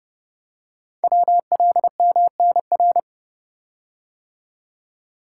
Morse code, heard as a keyed tone, about 30 wpm, 700 Hz. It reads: WLMNR
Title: Code Groups - 5 Letters - 30wpm